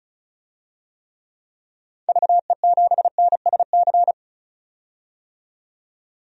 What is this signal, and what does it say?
Morse code, heard as a keyed tone, about 36 wpm, 700 Hz. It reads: VE7NSC
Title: DXCC-100 Call Signs - 6 Characters - 35wpm